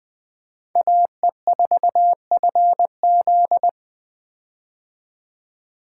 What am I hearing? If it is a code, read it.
Morse code, heard as a keyed tone, about 20 wpm, 700 Hz. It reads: AE4FZ
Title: 2023 POTA Hunter Call Signs - 20wpm